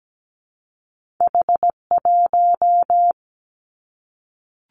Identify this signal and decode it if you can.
Morse code, heard as a keyed tone, about 17 wpm, 700 Hz. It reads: H1
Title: Two Challenging Letter-Number Combinations 17wpm